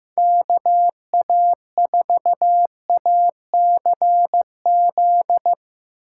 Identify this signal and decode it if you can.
Morse code, heard as a keyed tone, about 16 wpm, 700 Hz. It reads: KA4ACZ
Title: Speed Racing - USA 2x3 Call Signs 15wpm